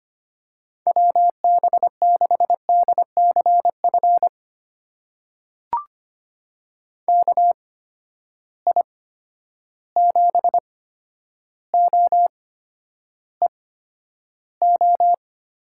Morse code, heard as a keyed tone, about 25 wpm, spaced as for 11 wpm, 700 Hz. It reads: WB6DCF KI7OEO
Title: USA Call Signs - 6 Characters - 8x Character Spacing - 25wpm